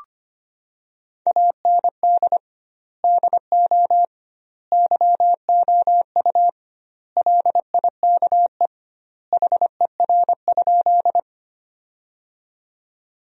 Morse code, heard as a keyed tone, about 25 wpm, 700 Hz. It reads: AND DO YOU LIKE HER?
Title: 2x Word-Spacing - Sentences from Top 100 Words - 25wpm